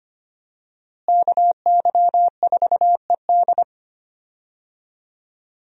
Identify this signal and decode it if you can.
Morse code, heard as a keyed tone, about 25 wpm, 700 Hz. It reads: KY4ED
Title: USA Call Signs - 5 Characters - Repeated 2x - 25wpm